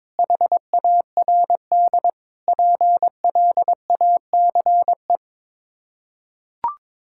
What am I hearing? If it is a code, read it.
Morse code, heard as a keyed tone, about 22 wpm, 700 Hz. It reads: HARD PLACE
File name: practice / 22wpm / Sets of 7 Words - 22wpm